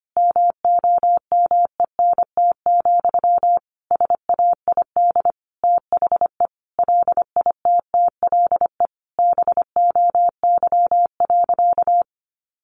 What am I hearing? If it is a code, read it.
Morse code, heard as a keyed tone, about 25 wpm, 700 Hz. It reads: MOMENT, SAID THE LITTLE BOY.
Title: Sentences from Top 1000 Words 25wpm